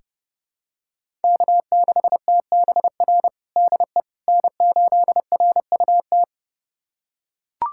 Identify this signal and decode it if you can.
Morse code, heard as a keyed tone, about 32 wpm, 700 Hz. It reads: K6TBR DE N8RUT
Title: USA Callsign DE Callsign - 5 Characters - 30wpm